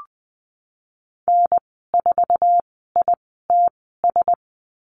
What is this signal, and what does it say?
Morse code, heard as a keyed tone, about 20 wpm, spaced as for 15 wpm, 700 Hz. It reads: N4ITS
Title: 069 Lesson 12g - Letter d - Call Signs 20wpm